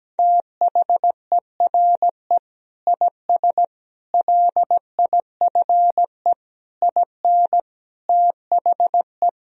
Morse code, heard as a keyed tone, about 17 wpm, 700 Hz. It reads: THERE IS LIFE IN THE
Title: Sentences from Top 1000 Words - Review Part 1 - 17wpm